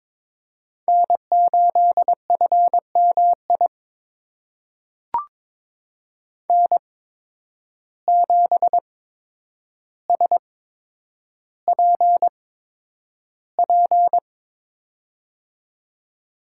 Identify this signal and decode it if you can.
Morse code, heard as a keyed tone, about 22 wpm, spaced as for 9 wpm, 700 Hz. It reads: N8FMI N7SPP
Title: USA Call Signs - 5 Characters - 8x Character Spacing - 22wpm